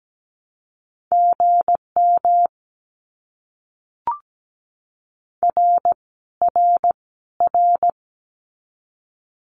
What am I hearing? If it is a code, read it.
Morse code, heard as a keyed tone, about 17 wpm, 700 Hz. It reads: GM R R R
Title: Abbreviations Repeated 3x - 17wpm